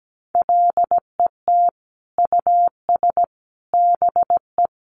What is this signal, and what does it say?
Morse code, heard as a keyed tone, about 17 wpm, 700 Hz. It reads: LET US BE
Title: Sentences from Top 1300 Words 17wpm